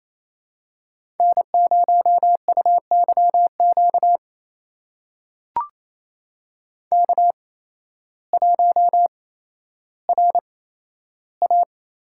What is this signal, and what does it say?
Morse code, heard as a keyed tone, about 28 wpm, spaced as for 13 wpm, 700 Hz. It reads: N0UYQ K1RA
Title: USA Call Signs - 5 Characters - 8x Character Spacing - 28wpm